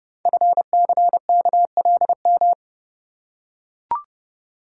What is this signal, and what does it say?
Morse code, heard as a keyed tone, about 28 wpm, 700 Hz. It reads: FCKLM
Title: Code Groups - 5 Letters - 30wpm